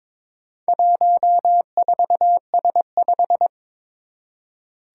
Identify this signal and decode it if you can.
Morse code, heard as a keyed tone, about 23 wpm, 700 Hz. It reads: 14S5
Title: Four Challenging Letter-Number Combinations 22wpm